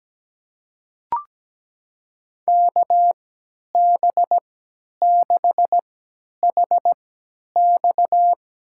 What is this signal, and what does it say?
Morse code, heard as a keyed tone, about 17 wpm, spaced as for 10 wpm, 700 Hz. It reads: KB6HX
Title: USA Call Signs - 6 Characters - 3x Character Spacing - 17wpm